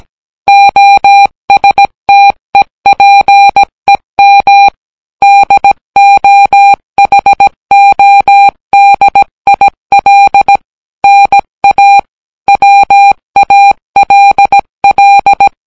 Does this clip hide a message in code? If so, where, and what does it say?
Yes: Morse code from 0.5 to 15.5 s, heard as a keyed tone, about 17 wpm, 800 Hz. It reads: OSTEPEM DOHODIL NA WALL